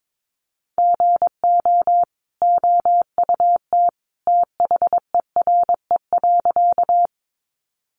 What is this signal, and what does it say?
Morse code, heard as a keyed tone, about 22 wpm, 700 Hz. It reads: GO OUT THERE.
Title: A Sound Of Thunder 22wpm